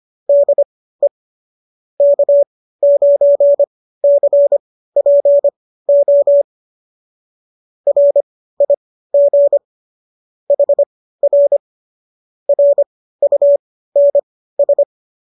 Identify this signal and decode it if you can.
Morse code, heard as a keyed tone, about 25 wpm, spaced as for 15 wpm, 570 Hz. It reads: DE K9CPO RIG HR RUNS
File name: qso211 15